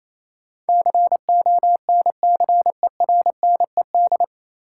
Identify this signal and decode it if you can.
Morse code, heard as a keyed tone, about 28 wpm, 700 Hz. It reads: CONCERNED
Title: Sentences from Top 1000 Words - Review Part 6 - New Word Drill - 28wpm